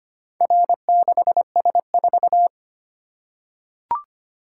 Morse code, heard as a keyed tone, about 25 wpm, 700 Hz. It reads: R6S4
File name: practice / 25wpm / Four Letter-Number Combinations - 25wpm